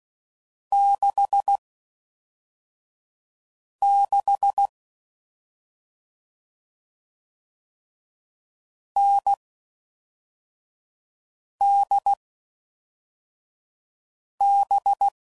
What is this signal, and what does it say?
Morse code, heard as a keyed tone, about 16 wpm, 800 Hz. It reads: 6 6 N D B
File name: Combinations / Difficult Combos / NDB NDB6